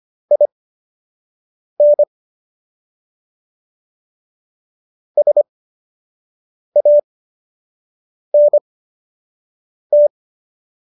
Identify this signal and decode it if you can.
Morse code, heard as a keyed tone, about 25 wpm, spaced as for 6 wpm, 590 Hz. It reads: IN SANT